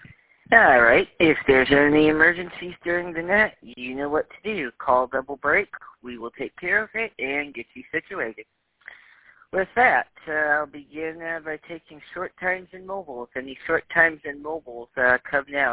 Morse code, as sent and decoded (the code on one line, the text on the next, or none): none